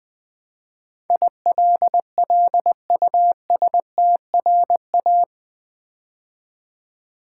Morse code, heard as a keyed tone, about 20 wpm, 700 Hz. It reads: ILLUSTRA
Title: Build A Word - 4 Letters At A Time - 12-Letter Words - 20wpm